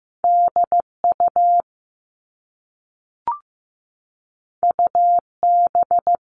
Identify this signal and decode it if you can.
Morse code, heard as a keyed tone, about 15 wpm, 700 Hz. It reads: DU UB
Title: All Cut Numbers - 2-Digit 15wpm